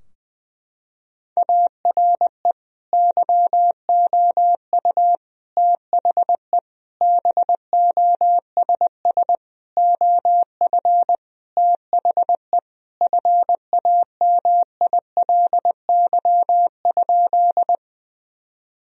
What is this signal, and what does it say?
Morse code, heard as a keyed tone, about 20 wpm, 700 Hz. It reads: ARE YOU THE BOSS OF THE FAMILY?
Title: Sentences from Top 1000 Words - Review Part 4 - 20wpm